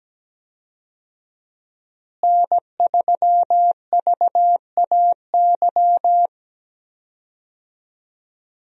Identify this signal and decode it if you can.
Morse code, heard as a keyed tone, about 17 wpm, 700 Hz. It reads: N3VAY
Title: USA Call Signs - 5 Characters - Repeated 2x - 17wpm